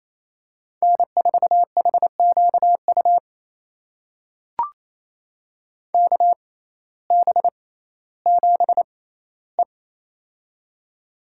Morse code, heard as a keyed tone, about 28 wpm, spaced as for 15 wpm, 700 Hz. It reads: N4HQU KB7E
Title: USA Call Signs - 6x Character Spacing - 28wpm